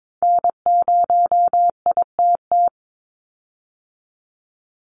Morse code, heard as a keyed tone, about 22 wpm, 700 Hz. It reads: N0ITT
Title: USA Call Signs 22wpm